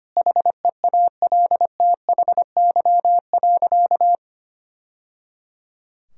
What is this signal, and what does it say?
Morse code, heard as a keyed tone, about 25 wpm, 700 Hz. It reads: HEALTHY.
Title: Sentences from Top 1000 Words - Review Part 5 - 25wpm